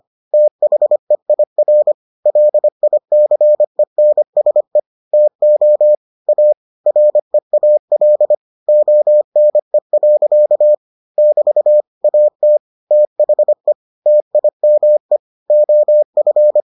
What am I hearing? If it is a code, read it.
Morse code, heard as a keyed tone, about 25 wpm, 600 Hz. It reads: THEIR LICENSE TO A REAL ONE. <BT> AT THE TIME OF